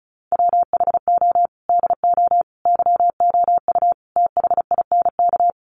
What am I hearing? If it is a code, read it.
Morse code, heard as a keyed tone, about 34 wpm, 700 Hz. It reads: WHO DO YOU THINK
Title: Sentences from Top 1500 Words 35wpm